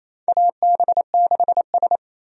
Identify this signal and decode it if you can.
Morse code, heard as a keyed tone, about 28 wpm, 700 Hz. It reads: AB6S
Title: USA Call Signs - 4 Characters - 28wpm